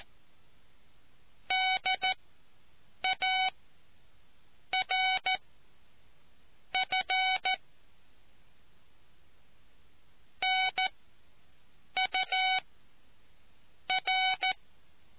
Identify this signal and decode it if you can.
Morse code, heard as a keyed tone, about 14 wpm, spaced as for 6 wpm, 750 Hz. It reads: DARF NUR